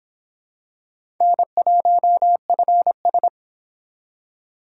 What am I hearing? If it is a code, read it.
Morse code, heard as a keyed tone, about 25 wpm, 700 Hz. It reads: N1FS